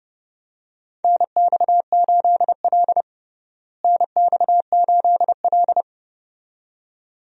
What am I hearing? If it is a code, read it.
Morse code, heard as a keyed tone, about 30 wpm, 700 Hz. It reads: NX8L NX8L
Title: USA Call Signs - 4 Characters - Repeated 2x - 30wpm